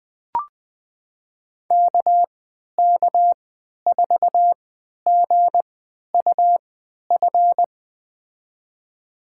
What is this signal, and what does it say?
Morse code, heard as a keyed tone, about 20 wpm, spaced as for 11 wpm, 700 Hz. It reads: KK4GUF
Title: USA Call Signs - 3x Character Spacing - 20wpm